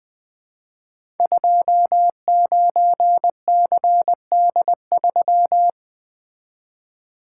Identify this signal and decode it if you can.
Morse code, heard as a keyed tone, about 20 wpm, 700 Hz. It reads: 29CD3
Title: Code Groups - 5 Letter:Numbers - 20wpm